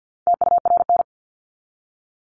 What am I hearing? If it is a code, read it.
Morse code, heard as a keyed tone, about 49 wpm, 700 Hz. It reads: TURN
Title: ING Suffix - Top 100 Words with Base Words 50wpm